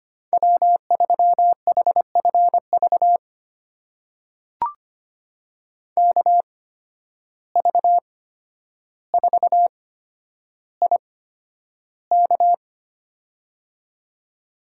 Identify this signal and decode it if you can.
Morse code, heard as a keyed tone, about 25 wpm, spaced as for 11 wpm, 700 Hz. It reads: W3HFV KV4IK
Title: USA Call Signs - 5 Characters - 8x Character Spacing - 25wpm